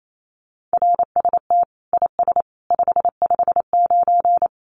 Morse code, heard as a keyed tone, about 28 wpm, 700 Hz. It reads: RST IS 559
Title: Common QSO Phrases 28wpm